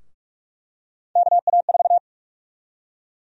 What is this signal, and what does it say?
Morse code, heard as a keyed tone, about 46 wpm, 700 Hz. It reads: KA4
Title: USA Call Sign Prefixes - Repeated 2x - 45wpm